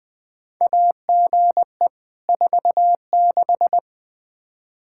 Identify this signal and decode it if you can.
Morse code, heard as a keyed tone, about 20 wpm, 700 Hz. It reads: AGE 46
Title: Practice QSO - Exchange 3 - More Info - 20wpm